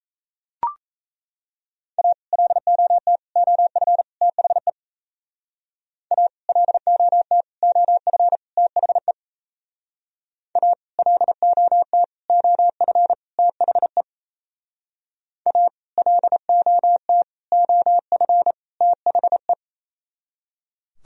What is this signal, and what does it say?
Morse code, heard as a keyed tone, about 34 wpm, 700 Hz. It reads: A LOT OF THE A LOT OF THE A LOT OF THE A LOT OF THE
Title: Speed-Racing - Sets of 4 Words 28wpm